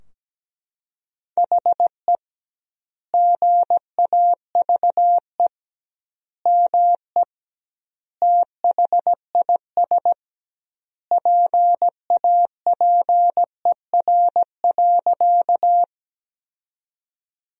Morse code, heard as a keyed tone, about 17 wpm, 700 Hz. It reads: HE GAVE ME THIS PAPER.
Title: 2x Word-Spacing - Sentences from Top 500 Words - 17wpm